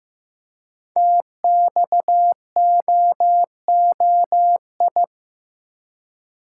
Morse code, heard as a keyed tone, about 15 wpm, 700 Hz. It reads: TXOOI
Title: Code Groups - 5 Letters - 15wpm